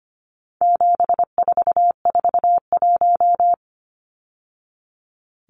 Morse code, heard as a keyed tone, about 25 wpm, 700 Hz. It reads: 7441
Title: Numbers - 4-Digit 25wpm